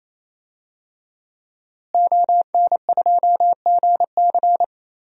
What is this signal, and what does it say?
Morse code, heard as a keyed tone, about 28 wpm, 700 Hz. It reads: ON2GC
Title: DXCC-100 Call Signs - 5 Characters - Repeated 3x - 28wpm